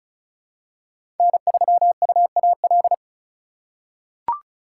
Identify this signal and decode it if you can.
Morse code, heard as a keyed tone, about 36 wpm, 700 Hz. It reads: N3UAL
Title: USA Call Signs - 5 Characters - 3x Character Spacing - 35wpm